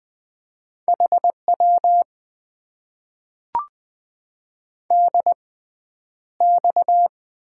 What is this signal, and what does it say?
Morse code, heard as a keyed tone, about 20 wpm, spaced as for 10 wpm, 700 Hz. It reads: HW DX
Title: Top 90 QSO Elements - 6x Character Spacing - 20wpm